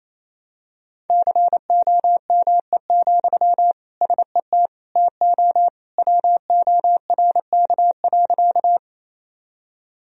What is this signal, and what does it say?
Morse code, heard as a keyed tone, about 28 wpm, 700 Hz. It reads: COME, SET TO WORK.
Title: Sentences from Top 1000 Words - Review Part 1 - 28wpm